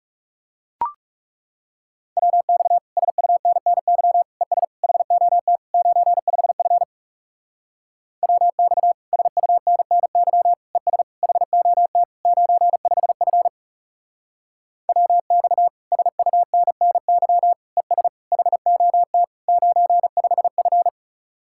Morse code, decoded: WX SUNNY ES HOT 95F WX SUNNY ES HOT 95F WX SUNNY ES HOT 95F

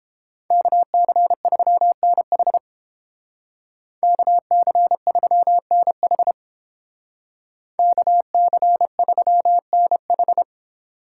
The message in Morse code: KC3NH KC3NH KC3NH

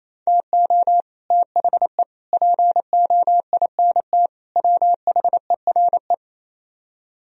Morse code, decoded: TO THE POINT WHERE

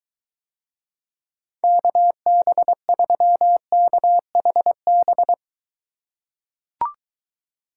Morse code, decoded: KB3KHB

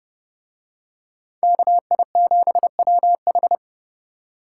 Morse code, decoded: KI7WH

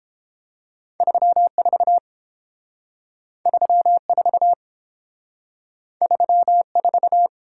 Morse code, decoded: 34 34 34

